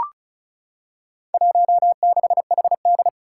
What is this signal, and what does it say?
Morse code, heard as a keyed tone, about 36 wpm, 700 Hz. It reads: 16HD
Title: Four Challenging Letter-Number Combinations 35wpm